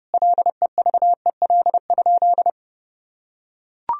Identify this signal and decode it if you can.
Morse code, heard as a keyed tone, about 32 wpm, 700 Hz. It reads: LEVEL?
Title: Sentences from Top 1000 Words - Review Part 4 - 30wpm